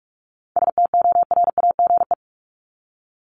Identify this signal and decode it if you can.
Morse code, heard as a keyed tone, about 44 wpm, 700 Hz. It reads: STORAGE
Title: Top 1251-1500 Words 45wpm